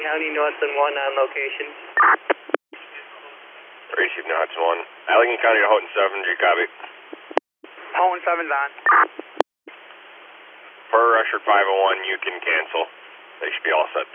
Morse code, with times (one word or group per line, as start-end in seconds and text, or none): none